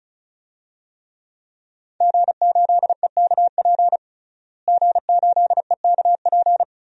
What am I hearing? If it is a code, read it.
Morse code, heard as a keyed tone, about 36 wpm, 700 Hz. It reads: G8EKP G8EKP
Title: DXCC-100 Call Signs - 5 Characters - Repeated 3x - 35wpm